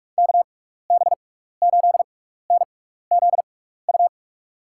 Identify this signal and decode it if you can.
Morse code, heard as a keyed tone, about 46 wpm, spaced as for 15 wpm, 700 Hz. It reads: KB8NZU
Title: USA Call Signs - 6x Character Spacing - 45wpm